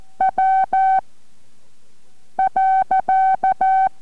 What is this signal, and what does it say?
Morse code, heard as a keyed tone, about 14 wpm, spaced as for 5 wpm, 760 Hz. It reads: W.
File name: MorseCode-5WPM 3